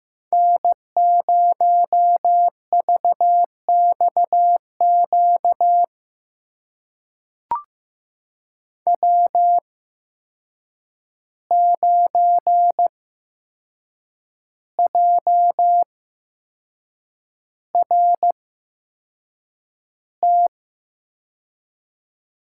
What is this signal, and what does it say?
Morse code, heard as a keyed tone, about 15 wpm, spaced as for 6 wpm, 700 Hz. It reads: N0VXQ W9JRT
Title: USA Call Signs - 5 Characters - 8x Character Spacing - 15wpm